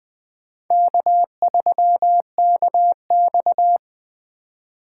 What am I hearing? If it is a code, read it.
Morse code, heard as a keyed tone, about 21 wpm, 700 Hz. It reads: K3KX
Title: Speed Racing - USA 1x2 Call Signs 17wpm